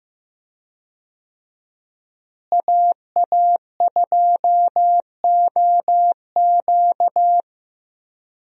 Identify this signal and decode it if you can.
Morse code, heard as a keyed tone, about 15 wpm, 700 Hz. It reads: AA2OQ
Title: USA Call Signs - 5 Characters - Repeated 2x - 15wpm